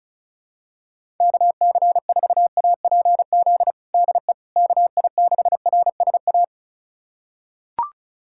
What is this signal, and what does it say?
Morse code, heard as a keyed tone, about 36 wpm, 700 Hz. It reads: KC4APZ DE KI6RSA
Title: USA Callsign DE Callsign - 6 Characters - 35wpm